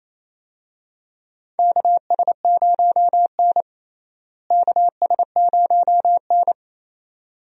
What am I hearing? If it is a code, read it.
Morse code, heard as a keyed tone, about 28 wpm, 700 Hz. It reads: KS0N KS0N
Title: USA Call Signs - 4 Characters - Repeated 2x - 28wpm